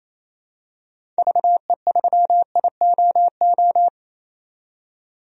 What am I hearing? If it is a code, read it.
Morse code, heard as a keyed tone, about 27 wpm, 700 Hz. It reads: VE3IOO
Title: DXCC-100 Call Signs - 6 Characters - 28wpm